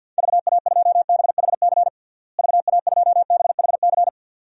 Morse code, heard as a keyed tone, about 49 wpm, 700 Hz. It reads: VA2BHC VA2BHC
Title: DXCC-100 Call Signs - 6 Characters - Repeated 2x - 50wpm